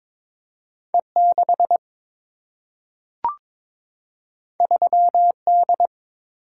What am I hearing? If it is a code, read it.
Morse code, heard as a keyed tone, about 22 wpm, 700 Hz. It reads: E6 3D
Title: Single-Letter Single-Number 22wpm